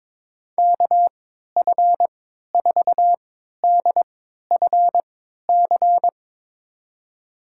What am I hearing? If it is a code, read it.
Morse code, heard as a keyed tone, about 22 wpm, spaced as for 13 wpm, 700 Hz. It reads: KF4DFC
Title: USA Call Signs - 6 Characters - 3x Character Spacing - 22wpm